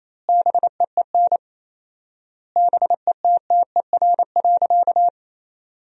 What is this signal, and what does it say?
Morse code, heard as a keyed tone, about 28 wpm, 700 Hz. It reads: BEEN BETTER.
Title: 4x Word-Spacing - Sentences from Top 500 Words - 28wpm